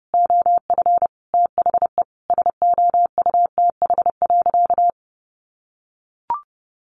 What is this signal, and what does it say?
Morse code, heard as a keyed tone, about 29 wpm, 700 Hz. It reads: OF THE SOUTH.
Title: Sentences from Top 1500 Words 30wpm